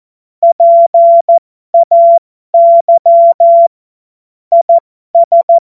Morse code, heard as a keyed tone, about 14 wpm, spaced as for 12 wpm, 680 Hz. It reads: PAY IS
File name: Morsekurs / Witze